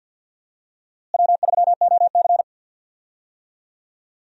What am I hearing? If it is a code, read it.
Morse code, heard as a keyed tone, about 51 wpm, 700 Hz. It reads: W3OC
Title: USA Call Signs - 4 Characters - Repeated 2x - 50wpm